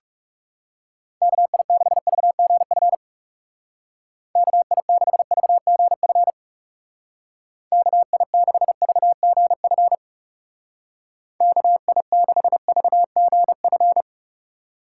Morse code, heard as a keyed tone, about 36 wpm, 700 Hz. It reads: KI6VGF KI6VGF KI6VGF KI6VGF
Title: Speed Racing - USA 2x3 Call Signs 30wpm